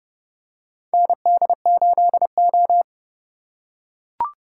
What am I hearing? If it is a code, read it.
Morse code, heard as a keyed tone, about 30 wpm, 700 Hz. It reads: ND8O
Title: USA Call Signs - 4 Characters - 4x Character Spacing - 30wpm